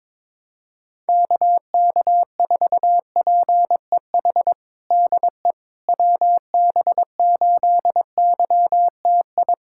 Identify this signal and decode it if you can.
Morse code, heard as a keyed tone, about 22 wpm, 700 Hz. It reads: KK4PEH DE WB8YTI
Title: USA Callsign DE Callsign - 6 Characters - 22wpm